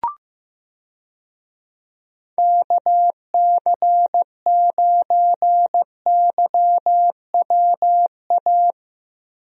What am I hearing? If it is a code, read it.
Morse code, heard as a keyed tone, about 16 wpm, 700 Hz. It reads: KC9YWA
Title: USA Call Signs - 6 Characters - Repeated 2x - 15wpm